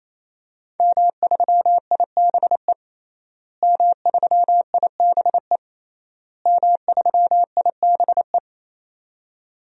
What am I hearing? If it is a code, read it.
Morse code, heard as a keyed tone, about 28 wpm, 700 Hz. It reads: M3IBE M3IBE M3IBE
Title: DXCC-100 Call Signs - 5 Characters - Repeated 3x - 28wpm